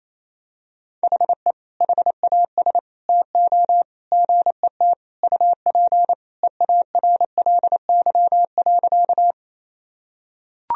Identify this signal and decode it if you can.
Morse code, heard as a keyed tone, about 28 wpm, 700 Hz. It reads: HE HAS TO GET UP EARLY.
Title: Sentences from Top 1000 Words - Review Part 2 - 28wpm